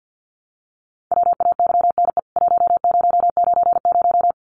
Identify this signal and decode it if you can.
Morse code, heard as a keyed tone, about 53 wpm, 700 Hz. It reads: WAYNE 1090
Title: CWT Exchange 50wpm